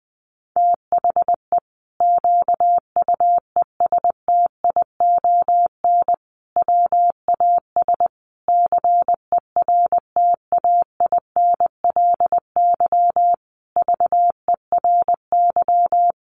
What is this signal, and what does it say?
Morse code, heard as a keyed tone, about 20 wpm, 700 Hz. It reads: THE QUESTION WAS CERTAINLY VERY
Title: Sentences from Top 1200 Words 20wpm